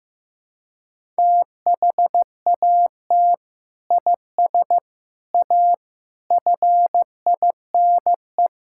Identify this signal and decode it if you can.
Morse code, heard as a keyed tone, about 15 wpm, 700 Hz. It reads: THAT IS A FINE
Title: Sentences from Top 1000 Words - Review Part 5 - 15wpm